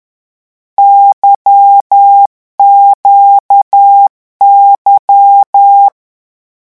Morse code, heard as a keyed tone, about 10 wpm, 800 Hz. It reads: YQY
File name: morse mp3 / lecon2-vitesse 10